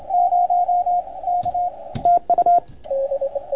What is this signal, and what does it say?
Morse code, heard as a keyed tone, about 28 wpm, 700 Hz, with radio noise. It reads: 0 K TU
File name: Archiv / CUC287 / OK1FGD CUC 287